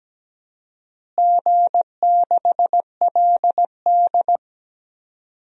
Morse code, heard as a keyed tone, about 17 wpm, 700 Hz. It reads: G6LD
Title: DXCC-100 Call Signs - 4 Characters - Repeated 2x - 17wpm